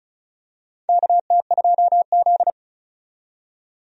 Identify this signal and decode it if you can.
Morse code, heard as a keyed tone, about 36 wpm, 690 Hz. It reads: KT2Z
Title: USA Call Signs - 4 Characters - 35wpm